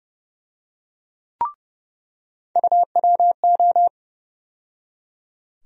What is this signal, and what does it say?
Morse code, heard as a keyed tone, about 30 wpm, 700 Hz. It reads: UWO
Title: USA Call Sign Postfixes - 3 Letters 30wpm